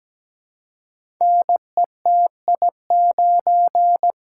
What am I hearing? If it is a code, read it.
Morse code, heard as a keyed tone, about 17 wpm, 700 Hz. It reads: NETI9
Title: Code Groups - 5 Letter:Numbers - 17wpm